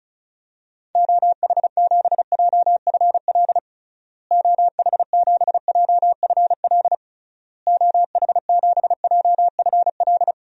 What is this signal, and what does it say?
Morse code, heard as a keyed tone, about 34 wpm, 700 Hz. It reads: OH7JFL OH7JFL OH7JFL
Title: DXCC-100 Call Signs - 6 Characters - Repeated 3x - 35wpm